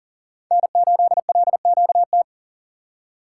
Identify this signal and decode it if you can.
Morse code, heard as a keyed tone, about 40 wpm, 700 Hz. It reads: N8LQT
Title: USA Call Signs - 5 Characters - 2x Character Spacing - 40wpm